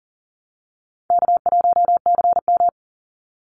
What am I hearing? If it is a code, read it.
Morse code, heard as a keyed tone, about 40 wpm, 700 Hz. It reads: K1CM
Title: NA SOTA Chaser Callsigns 40wpm